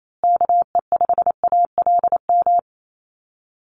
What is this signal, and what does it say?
Morse code, heard as a keyed tone, about 28 wpm, 700 Hz. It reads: KE5ALM
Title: USA Call Signs Repeated 3x 28wpm